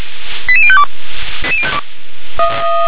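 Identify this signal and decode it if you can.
Five-tone selective call (ZVEI), heard as quick runs of five tones, tones of about 70 ms. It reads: 88052
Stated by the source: ZVEI five-tone selective call 88163 received on an SDR.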